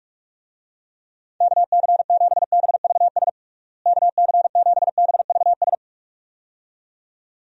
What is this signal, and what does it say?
Morse code, heard as a keyed tone, about 43 wpm, 700 Hz. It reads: KC7BVS KC7BVS
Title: USA Call Signs - 6 Characters - Repeated 2x - 45wpm